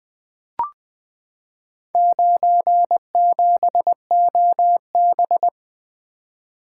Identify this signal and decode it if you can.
Morse code, heard as a keyed tone, about 20 wpm, 700 Hz. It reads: 97OB
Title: Four Letter-Number Combinations - 20wpm